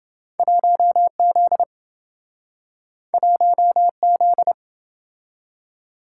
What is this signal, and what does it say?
Morse code, heard as a keyed tone, about 27 wpm, 700 Hz. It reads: 1Z 1Z